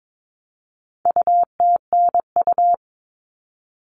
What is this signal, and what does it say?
Morse code, heard as a keyed tone, about 22 wpm, 700 Hz. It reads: UTNU
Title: All Cut Numbers - 4-Digit 22wpm